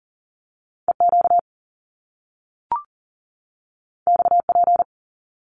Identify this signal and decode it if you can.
Morse code, heard as a keyed tone, about 42 wpm, 700 Hz. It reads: EQ XP
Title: Top Two Letter Combinations in Words 40wpm